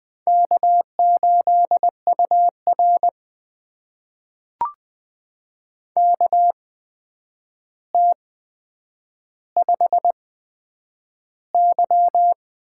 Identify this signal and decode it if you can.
Morse code, heard as a keyed tone, about 21 wpm, spaced as for 9 wpm, 700 Hz. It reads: K8UR KT5Y
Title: USA Call Signs - 4 Characters - 8x Character Spacing - 20wpm